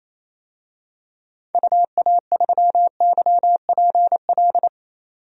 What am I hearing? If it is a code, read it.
Morse code, heard as a keyed tone, about 27 wpm, 700 Hz. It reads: UA3YPL
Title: DXCC-100 Call Signs - 6 Characters - 28wpm